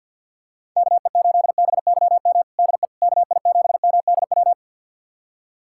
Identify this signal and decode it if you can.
Morse code, heard as a keyed tone, about 52 wpm, 690 Hz. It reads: KE8BYM DE KI7MDW
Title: USA Callsign DE Callsign - 6 Characters - 50wpm